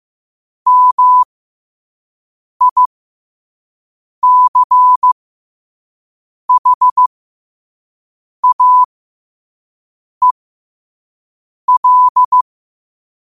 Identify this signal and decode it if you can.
Morse code, heard as a keyed tone, about 15 wpm, spaced as for 5 wpm, 1000 Hz. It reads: MICHAEL
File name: qotdfiles / 5wpm